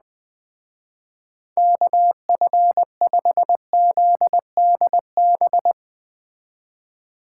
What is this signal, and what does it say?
Morse code, heard as a keyed tone, about 20 wpm, 700 Hz. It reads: KF5ZDB
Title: Speed Racing - USA 2x3 Call Signs 17wpm